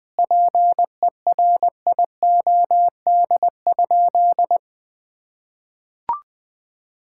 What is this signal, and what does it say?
Morse code, heard as a keyed tone, about 19 wpm, 700 Hz. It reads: PERIOD?
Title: Sentences from Top 1000 Words - Review Part 5 - 20wpm